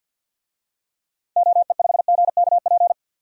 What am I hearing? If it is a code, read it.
Morse code, heard as a keyed tone, about 47 wpm, 700 Hz. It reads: OE5GKP
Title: DXCC-100 Call Signs - 6 Characters - 50wpm